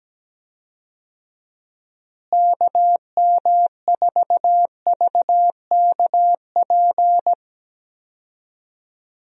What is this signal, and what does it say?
Morse code, heard as a keyed tone, about 17 wpm, 700 Hz. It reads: KM4VKP
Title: USA Call Signs - 6 Characters - Repeated 2x - 17wpm